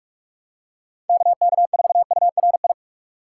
Morse code, heard as a keyed tone, about 42 wpm, 700 Hz. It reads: KK4URI
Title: USA Call Signs - 4x Character Spacing - 45wpm